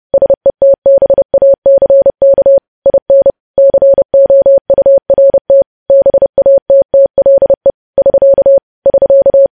Morse code, heard as a keyed tone, about 30 wpm, 550 Hz. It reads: SETBACK IN COURT BATTLE <SK> <SK>
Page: news headlines in morse code at 30 wpm